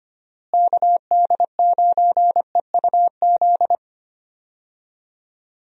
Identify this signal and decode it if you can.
Morse code, heard as a keyed tone, about 25 wpm, 700 Hz. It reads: KD9EUZ
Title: USA Call Signs - 6 Characters - Repeated 3x - 25wpm